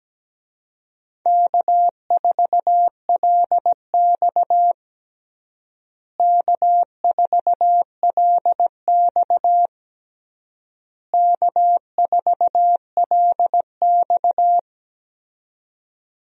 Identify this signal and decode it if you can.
Morse code, heard as a keyed tone, about 17 wpm, 700 Hz. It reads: K4LX K4LX K4LX
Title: USA Call Signs - 4 Characters - Repeated 3x - 17wpm